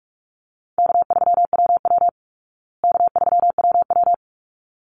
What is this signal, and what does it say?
Morse code, heard as a keyed tone, about 44 wpm, 700 Hz. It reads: K3WW K3WW
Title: Easy Call Signs Repeated 2x - 45wpm